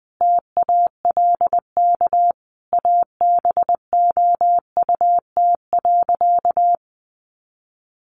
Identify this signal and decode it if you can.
Morse code, heard as a keyed tone, about 20 wpm, 700 Hz. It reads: TALK ABOUT.
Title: Sentences from Top 800 Words 20wpm